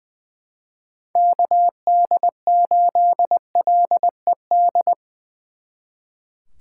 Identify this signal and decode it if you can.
Morse code, heard as a keyed tone, about 22 wpm, 700 Hz. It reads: KD8LED
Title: Speed Racing - USA 2x3 Call Signs 20wpm